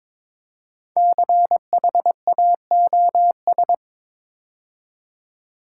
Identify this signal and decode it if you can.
Morse code, heard as a keyed tone, about 22 wpm, 700 Hz. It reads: CHAOS
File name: 5 Letter Words - First 3 Letters Given - 22wpm